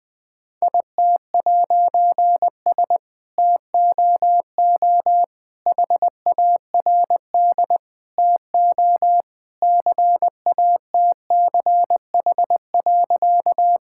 Morse code, heard as a keyed tone, about 20 wpm, 700 Hz. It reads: IT'S TOO HARD TO CATCH.
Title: All-Age Appropriate Jokes - 20wpm